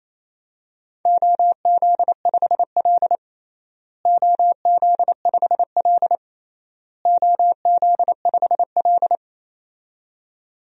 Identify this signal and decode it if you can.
Morse code, heard as a keyed tone, about 28 wpm, 700 Hz. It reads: OZ5L OZ5L OZ5L
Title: DXCC-100 Call Signs - 4 Characters - Repeated 3x - 28wpm